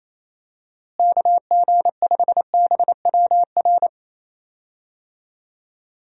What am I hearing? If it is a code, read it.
Morse code, heard as a keyed tone, about 28 wpm, 700 Hz. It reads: KG5BWR
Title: USA Call Signs - 6 Characters - Repeated 3x - 28wpm